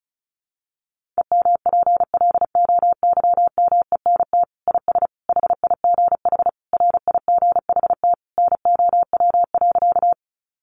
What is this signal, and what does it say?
Morse code, heard as a keyed tone, about 34 wpm, 700 Hz. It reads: EMPLOYMENT IS HIGH RIGHT NOW.
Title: Sentences from Top 1500 Words 35wpm